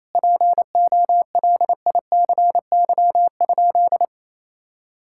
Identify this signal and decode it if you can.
Morse code, heard as a keyed tone, about 28 wpm, 700 Hz. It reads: POLICY?
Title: Sentences from Top 1000 Words - Review Part 5 - 28wpm